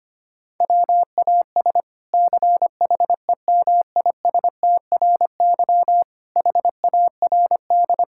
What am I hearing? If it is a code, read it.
Morse code, heard as a keyed tone, about 25 wpm, 700 Hz. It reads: WAS CHEMISTRY HARD
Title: Sentences from Top 1000 Words - Review Part 6 - 25wpm